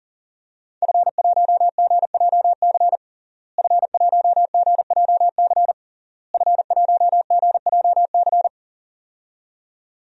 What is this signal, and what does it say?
Morse code, heard as a keyed tone, about 40 wpm, 700 Hz. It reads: F1GJC F1GJC F1GJC
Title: DXCC-100 Call Signs - 5 Characters - Repeated 3x - 40wpm